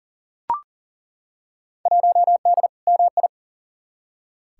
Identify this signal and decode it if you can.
Morse code, heard as a keyed tone, about 40 wpm, 700 Hz. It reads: 1D MI
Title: Field Day Exchanges - 40wpm